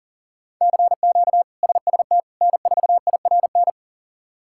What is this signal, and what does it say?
Morse code, heard as a keyed tone, about 40 wpm, 700 Hz. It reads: CQ SST N4IRN
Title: K1USN SST QSO Practice - 40wpm